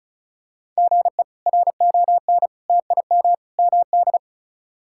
Morse code, heard as a keyed tone, about 34 wpm, 700 Hz. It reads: GE RON TIM MD
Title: K1USN SST QSO Practice - 35wpm